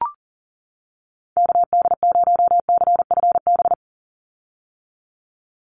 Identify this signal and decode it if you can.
Morse code, heard as a keyed tone, about 40 wpm, 700 Hz. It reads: KD0CFB
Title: USA Call Signs 40wpm